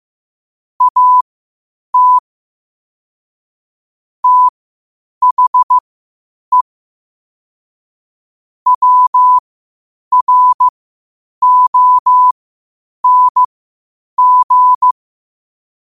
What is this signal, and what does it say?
Morse code, heard as a keyed tone, about 15 wpm, spaced as for 8 wpm, 1000 Hz. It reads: AT THE WRONG
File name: qotdfiles / 8wpm